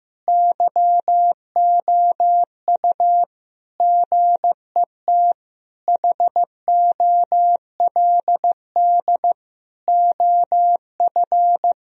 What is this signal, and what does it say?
Morse code, heard as a keyed tone, about 15 wpm, 700 Hz. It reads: YOU GET HOLD OF
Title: Sentences from Top 1000 Words - Review Part 2 - 15wpm